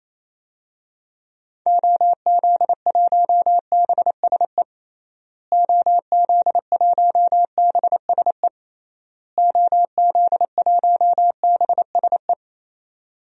Morse code, heard as a keyed tone, about 28 wpm, 700 Hz. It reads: OZ1BSE OZ1BSE OZ1BSE
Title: DXCC-100 Call Signs - 6 Characters - Repeated 3x - 28wpm